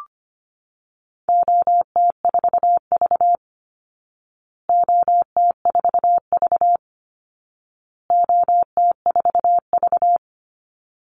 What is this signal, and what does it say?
Morse code, heard as a keyed tone, about 25 wpm, 700 Hz. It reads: OT4V OT4V OT4V
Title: Call Signs Repeated 2x - 25wpm